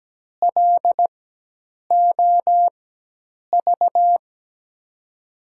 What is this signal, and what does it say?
Morse code, heard as a keyed tone, about 17 wpm, spaced as for 8 wpm, 700 Hz. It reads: LOV
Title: USA Call Signs - 4x Character Spacing - 17wpm